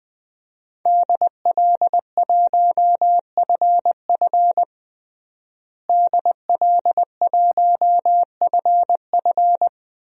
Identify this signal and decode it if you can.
Morse code, heard as a keyed tone, about 20 wpm, 700 Hz. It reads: DL1FF DL1FF
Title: DXCC-100 Call Signs - 5 Characters - Repeated 2x - 20wpm